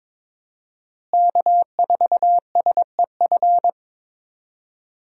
Morse code, heard as a keyed tone, about 22 wpm, 700 Hz. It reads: K4SEF
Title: USA Call Signs - 5 Characters - Repeated 2x - 22wpm